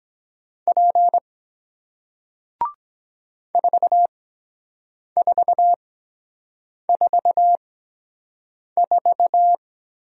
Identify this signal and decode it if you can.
Morse code, heard as a keyed tone, about 23 wpm, spaced as for 7 wpm, 700 Hz. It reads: P 4444